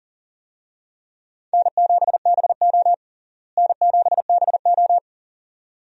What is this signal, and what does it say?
Morse code, heard as a keyed tone, about 40 wpm, 700 Hz. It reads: N7BO N7BO
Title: USA Call Signs - 4 Characters - Repeated 2x - 40wpm